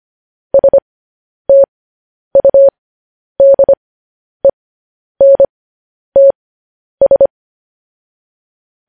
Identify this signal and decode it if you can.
Morse code, heard as a keyed tone, about 25 wpm, spaced as for 10 wpm, 550 Hz. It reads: STUDENTS
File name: podcast / News headlines-1x-25x10